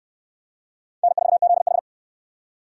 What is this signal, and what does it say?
Morse code, heard as a keyed tone, about 67 wpm, 700 Hz. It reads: N4ZH